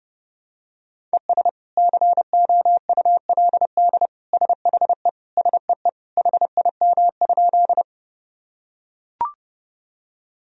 Morse code, heard as a keyed tone, about 30 wpm, 700 Hz. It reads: ES COULD SHE SEE HIM?
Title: Sentences from Top 100 Words and Abbreviations 30wpm